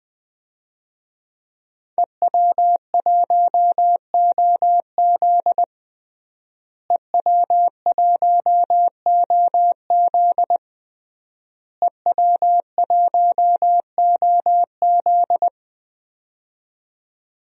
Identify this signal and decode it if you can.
Morse code, heard as a keyed tone, about 21 wpm, 700 Hz. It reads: EW1OZ EW1OZ EW1OZ
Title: DXCC-100 Call Signs - 5 Characters - Repeated 3x - 20wpm